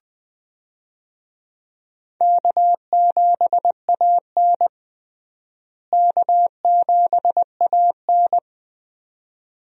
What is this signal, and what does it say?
Morse code, heard as a keyed tone, about 21 wpm, 700 Hz. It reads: K7AN K7AN
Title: USA Call Signs - 4 Characters - Repeated 2x - 20wpm